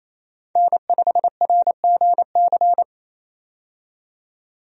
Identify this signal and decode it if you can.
Morse code, heard as a keyed tone, about 28 wpm, 700 Hz. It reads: N5RGC
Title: USA Call Signs - 5 Characters - 28wpm